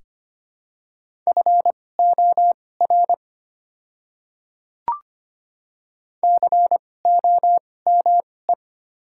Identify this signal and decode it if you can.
Morse code, heard as a keyed tone, about 25 wpm, spaced as for 18 wpm, 700 Hz. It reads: FOR COME
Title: 2x Character Spacing - Top 100 Words - 25wpm